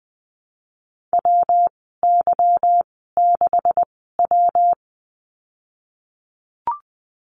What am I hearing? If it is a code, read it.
Morse code, heard as a keyed tone, about 20 wpm, spaced as for 14 wpm, 700 Hz. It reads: WY6W
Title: 161 Lesson 26g - Letter g - Call Signs 20wpm